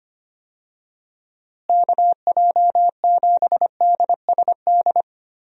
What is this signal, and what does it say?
Morse code, heard as a keyed tone, about 25 wpm, 700 Hz. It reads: KJ7DSD
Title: USA Call Signs - 6 Characters - Repeated 3x - 25wpm